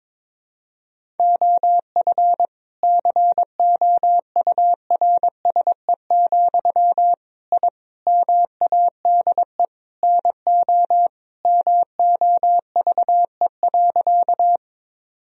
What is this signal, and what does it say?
Morse code, heard as a keyed tone, about 22 wpm, 700 Hz. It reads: OF COURSE, I MADE NO MOVE.
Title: Sentences from Top 1000 Words - Review Part 1 - 22wpm